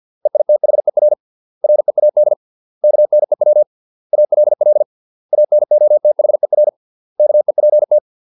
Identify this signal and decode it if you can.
Morse code, heard as a keyed tone, about 50 wpm, 600 Hz. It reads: EITHER READ KNEW ALL ANOTHER KEPT